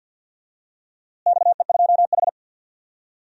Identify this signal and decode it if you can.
Morse code, heard as a keyed tone, about 51 wpm, 700 Hz. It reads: XE2H
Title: DXCC-100 Call Signs - 4 Characters - Repeated 2x - 50wpm